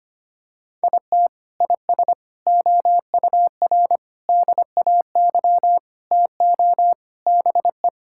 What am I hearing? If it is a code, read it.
Morse code, heard as a keyed tone, about 25 wpm, 700 Hz. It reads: IT IS OUR DAY TO BE
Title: Sentences from Top 1000 Words - Review Part 5 - 25wpm